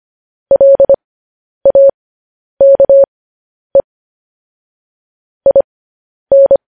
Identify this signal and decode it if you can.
Morse code, heard as a keyed tone, about 25 wpm, spaced as for 10 wpm, 550 Hz. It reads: LAKE IN
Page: news-headlines-in-morse-code-at-10-wpm